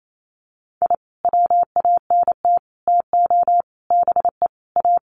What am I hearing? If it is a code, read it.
Morse code, heard as a keyed tone, about 28 wpm, 700 Hz. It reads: I WANT TO BE A
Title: Sets of 5 Words Encore 28wpm